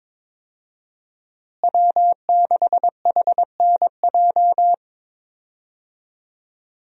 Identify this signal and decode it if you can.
Morse code, heard as a keyed tone, about 22 wpm, 700 Hz. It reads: W6HNJ
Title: USA Call Signs - 5 Characters - Repeated 3x - 22wpm